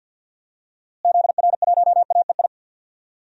Morse code, heard as a keyed tone, about 51 wpm, 700 Hz. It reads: ZR1AEI
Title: DXCC-100 Call Signs - 6 Characters - Repeated 3x - 50wpm